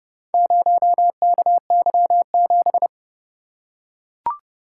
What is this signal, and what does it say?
Morse code, heard as a keyed tone, about 29 wpm, 700 Hz. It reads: 0KY7